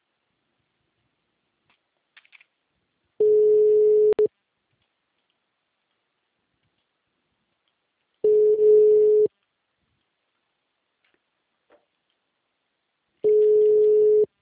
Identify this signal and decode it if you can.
Ringback tone (Europe, 425 Hz).